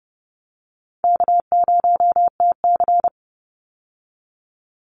Morse code, heard as a keyed tone, about 30 wpm, 700 Hz. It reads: K0TC
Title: Call Signs Repeated 2x - 30wpm